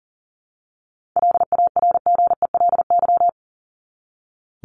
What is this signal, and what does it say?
Morse code, heard as a keyed tone, about 40 wpm, 700 Hz. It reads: LARGELY
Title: LY Suffix - Top 100 Words with Base Words 40wpm